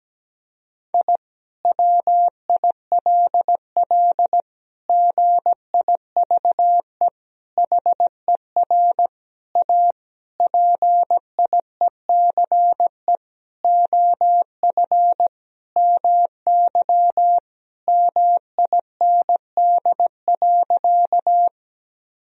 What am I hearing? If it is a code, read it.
Morse code, heard as a keyed tone, about 17 wpm, 700 Hz. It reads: I WILL GIVE HER A PIECE OF MY MIND.
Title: Sentences from Top 1000 Words - Review Part 2 - 17wpm